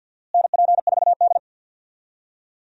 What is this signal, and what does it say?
Morse code, heard as a keyed tone, about 52 wpm, 700 Hz. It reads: NP4D
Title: USA Call Signs - 4 Characters - Repeated 3x - 50wpm